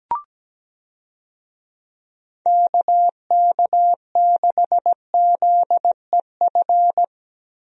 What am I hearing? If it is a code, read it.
Morse code, heard as a keyed tone, about 17 wpm, 700 Hz. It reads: KK6ZEF
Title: USA Call Signs - 6 Characters - Repeated 3x - 17wpm